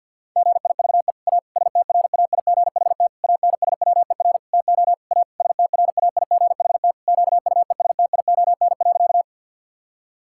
Morse code, heard as a keyed tone, about 50 wpm, 700 Hz. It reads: GIVE A STRAIGHT ANSWER TO A STRAIGHT QUESTION.
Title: Sentences from Top 1000 Words - Review Part 6 - 50wpm